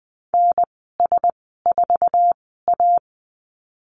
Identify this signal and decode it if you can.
Morse code, heard as a keyed tone, about 19 wpm, spaced as for 14 wpm, 700 Hz. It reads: NS4A
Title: 055 Lesson 10g - Letter r - Call Signs 20wpm